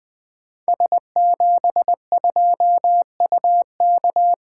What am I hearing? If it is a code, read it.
Morse code, heard as a keyed tone, about 20 wpm, 700 Hz. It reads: S72UK
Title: Code Groups - 5 Letter:Numbers - 20wpm